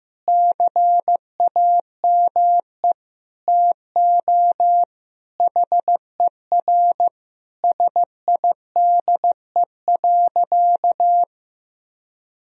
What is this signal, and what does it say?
Morse code, heard as a keyed tone, about 15 wpm, 700 Hz. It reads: CAME TO HER SIDE.
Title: Sentences from Top 1000 Words - Review Part 1 - 15wpm